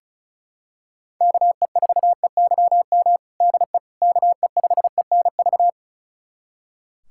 Morse code, heard as a keyed tone, about 34 wpm, 700 Hz. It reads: KE4EYM DE KE5ENV
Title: USA Callsign DE Callsign - 6 Characters - 35wpm